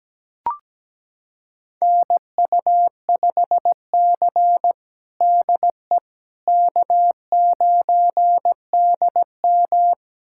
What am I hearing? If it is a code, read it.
Morse code, heard as a keyed tone, about 17 wpm, 700 Hz. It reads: NU5C DE K9DM
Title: USA Callsign DE Callsign - 4 Characters - 17wpm